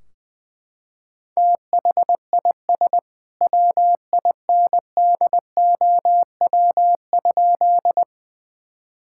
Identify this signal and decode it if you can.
Morse code, heard as a keyed tone, about 20 wpm, 700 Hz. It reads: THIS WINDOW?
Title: Sentences from Top 1600 Words 20wpm